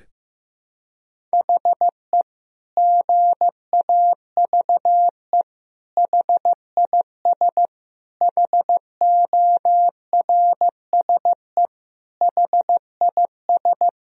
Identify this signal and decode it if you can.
Morse code, heard as a keyed tone, about 15 wpm, 700 Hz. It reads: HE GAVE HIS HORSE HIS
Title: Sentences from Top 1000 Words - Review Part 1 - 15wpm